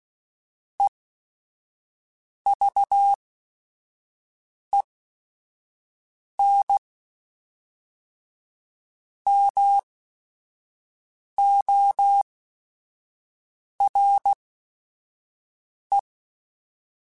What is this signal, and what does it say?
Morse code, heard as a keyed tone, about 16 wpm, spaced as for 5 wpm, 800 Hz. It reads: EVEN MORE
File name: Chuckles / Chuckles-mp3s / HumanErr @5